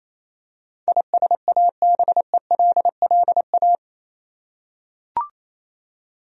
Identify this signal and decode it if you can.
Morse code, heard as a keyed tone, about 28 wpm, 700 Hz. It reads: ISABELLA
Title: USA Common Names - 28wpm